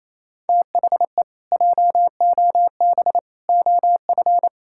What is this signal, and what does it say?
Morse code, heard as a keyed tone, about 28 wpm, 700 Hz. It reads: THE JOB OF
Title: Pangram Challenge v2 - 28wpm